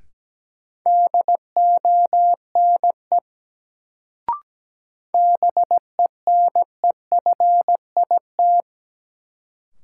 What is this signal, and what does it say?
Morse code, heard as a keyed tone, about 17 wpm, 700 Hz. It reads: DONE BENEFIT
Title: Sets of 5 Words Encore - New Word Drill - 17wpm